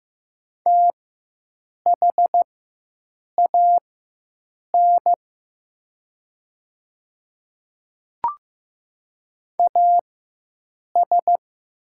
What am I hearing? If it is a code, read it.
Morse code, heard as a keyed tone, about 15 wpm, spaced as for 7 wpm, 700 Hz. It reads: THAN AS
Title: 4x Character Spacing - Top 500 Words - 15wpm